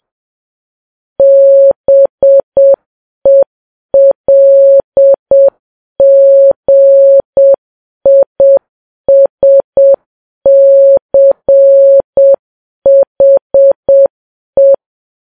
Morse code, heard as a keyed tone, about 7 wpm, 550 Hz. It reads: BELGISCHE
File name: Files Web tls / Morse on9fv / morse7kps